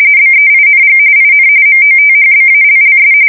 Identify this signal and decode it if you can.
RTTY, about 45.45 baud. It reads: BEING IN MOTHBALLS
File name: fsk170 / battleshipnj